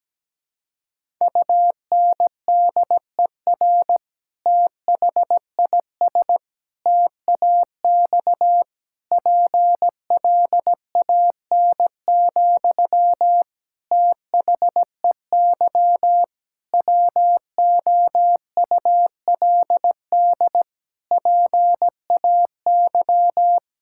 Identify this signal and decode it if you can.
Morse code, heard as a keyed tone, about 17 wpm, 700 Hz. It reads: UNDER THIS TAX PLAN, THEY WOULD PAY
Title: Sentences from Top 1000 Words - Review Part 5 - 17wpm